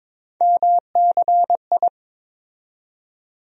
Morse code, heard as a keyed tone, about 22 wpm, 700 Hz. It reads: MCI
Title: USA Call Sign Postfixes - 3 Letters 22wpm